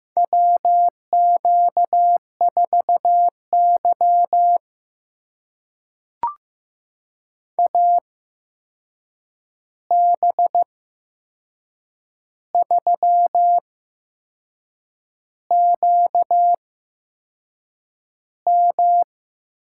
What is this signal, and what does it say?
Morse code, heard as a keyed tone, about 15 wpm, spaced as for 6 wpm, 700 Hz. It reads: WQ4Y AB3QM
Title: USA Call Signs - 8x Character Spacing - 15wpm